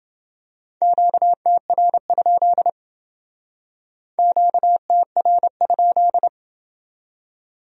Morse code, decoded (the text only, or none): QTR? QTR?